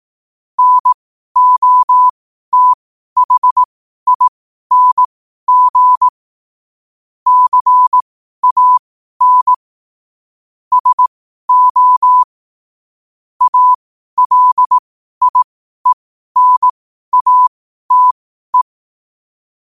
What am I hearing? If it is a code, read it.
Morse code, heard as a keyed tone, about 18 wpm, spaced as for 12 wpm, 1000 Hz. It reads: NOTHING CAN SO ALIENATE
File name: qotdfiles / 12wpm